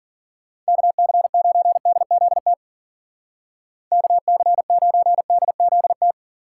Morse code, heard as a keyed tone, about 51 wpm, 700 Hz. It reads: KC9DZT KC9DZT